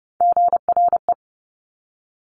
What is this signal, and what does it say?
Morse code, heard as a keyed tone, about 30 wpm, 700 Hz. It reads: GRE
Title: Top Three Letter Combinations in Words 30wpm